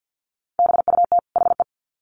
Morse code, heard as a keyed tone, about 50 wpm, 700 Hz. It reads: BUT HE
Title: Sets of 2 Words 50wpm